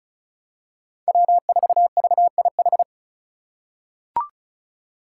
Morse code, heard as a keyed tone, about 36 wpm, 690 Hz. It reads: W4VIH
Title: Code Groups - 5 Letter:Numbers - 35wpm